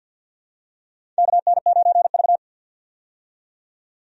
Morse code, heard as a keyed tone, about 54 wpm, 700 Hz. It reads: KN9V
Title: USA Call Signs - 4 Characters - 50wpm